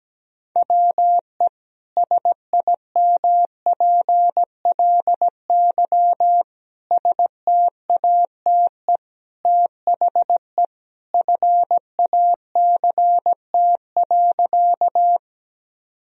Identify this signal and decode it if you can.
Morse code, heard as a keyed tone, about 17 wpm, 700 Hz. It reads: WE SIMPLY STATE THE FACT.
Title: Sentences from Top 1000 Words - Review Part 4 - 17wpm